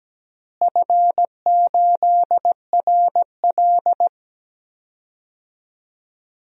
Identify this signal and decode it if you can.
Morse code, heard as a keyed tone, about 17 wpm, 700 Hz. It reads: F8RL
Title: DXCC-100 Call Signs - 4 Characters - Repeated 2x - 17wpm